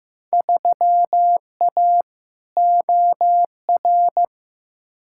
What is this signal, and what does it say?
Morse code, heard as a keyed tone, about 15 wpm, 700 Hz. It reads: 3A OR
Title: Field Day Exchanges - 15wpm